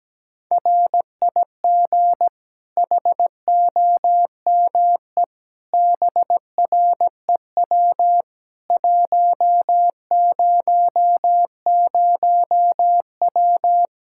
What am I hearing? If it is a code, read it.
Morse code, heard as a keyed tone, about 17 wpm, 700 Hz. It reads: RIG HOME BREW 100W
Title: K1USN SST QSO Practice 17wpm